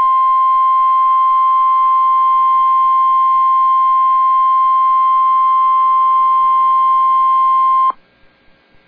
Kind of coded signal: Emergency Alert System (SAME) tones